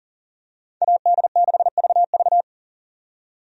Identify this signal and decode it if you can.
Morse code, heard as a keyed tone, about 40 wpm, 700 Hz. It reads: AD6VV
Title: USA Call Signs - 5 Characters - 40wpm